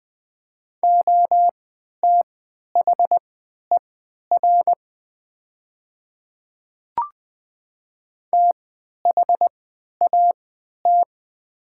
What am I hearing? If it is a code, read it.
Morse code, heard as a keyed tone, about 20 wpm, spaced as for 11 wpm, 700 Hz. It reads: OTHER THAT
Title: 3x Character Spacing - Top 100 Words - 20wpm